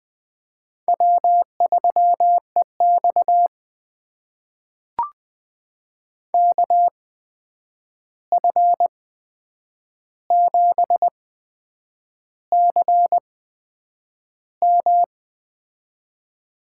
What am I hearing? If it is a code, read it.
Morse code, heard as a keyed tone, about 20 wpm, spaced as for 8 wpm, 700 Hz. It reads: W3EX KF7CM
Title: USA Call Signs - 8x Character Spacing - 20wpm